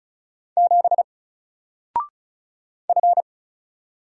Morse code, heard as a keyed tone, about 35 wpm, 700 Hz. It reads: 7 F
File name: Single-Letter-Number-35wpm